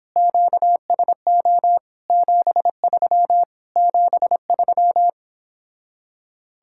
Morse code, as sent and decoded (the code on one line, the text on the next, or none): --.- ... --- / --... ...-- / --... ...--
QSO 73 73